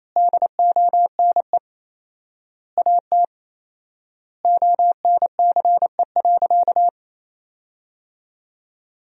Morse code, heard as a keyed tone, about 28 wpm, 700 Hz. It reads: DONE AT ONCE.
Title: 4x Word-Spacing - Sentences from Top 500 Words - 28wpm